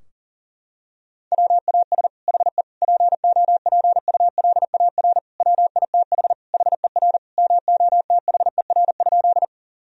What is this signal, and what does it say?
Morse code, heard as a keyed tone, about 37 wpm, 700 Hz. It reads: WAS HE POPULAR WITH HER MOTHER?
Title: Sentences from Top 1000 Words - Review Part 3 - 40wpm